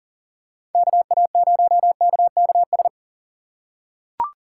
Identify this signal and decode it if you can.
Morse code, heard as a keyed tone, about 40 wpm, 700 Hz. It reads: KA0KKS
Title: USA Call Signs - 6x Character Spacing - 40wpm